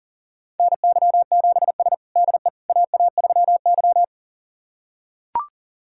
Morse code, heard as a keyed tone, about 40 wpm, 700 Hz. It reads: NY7S DE AA3Y
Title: USA Callsign DE Callsign - 4 Characters - 40wpm